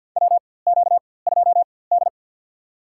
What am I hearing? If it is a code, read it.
Morse code, heard as a keyed tone, about 49 wpm, spaced as for 23 wpm, 700 Hz. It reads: WQ2D
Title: USA Call Signs - 4x Character Spacing - 50wpm